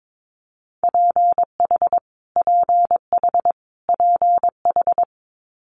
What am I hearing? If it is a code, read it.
Morse code, heard as a keyed tone, about 22 wpm, 700 Hz. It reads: PH PH PH
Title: Multi-letter Phonemes Repeated 3x 22wpm